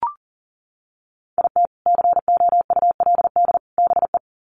Morse code, heard as a keyed tone, about 40 wpm, 700 Hz. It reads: IT COULD BE
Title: Sets of 3 Words 40wpm